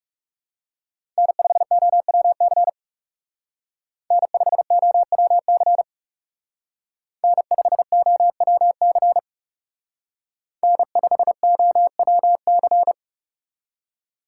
Morse code, heard as a keyed tone, about 38 wpm, 700 Hz. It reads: N5OWC N5OWC N5OWC N5OWC